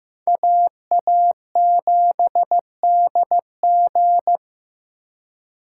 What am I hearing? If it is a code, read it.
Morse code, heard as a keyed tone, about 18 wpm, 700 Hz. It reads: AA7DG
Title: Speed Racing - USA 2x2 Call Signs 15wpm